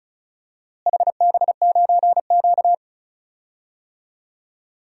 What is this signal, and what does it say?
Morse code, heard as a keyed tone, about 36 wpm, 700 Hz. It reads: HB9Q
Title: DXCC-100 Call Signs - 4 Characters - Repeated 2x - 35wpm